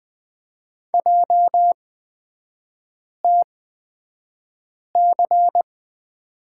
Morse code, heard as a keyed tone, about 20 wpm, spaced as for 5 wpm, 690 Hz. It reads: JTC
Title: 229 Lesson 37e - Letter x - Three Characters 20wpm